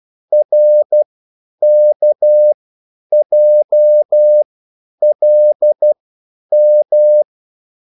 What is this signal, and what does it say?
Morse code, heard as a keyed tone, about 12 wpm, spaced as for 9 wpm, 600 Hz. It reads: RKJLM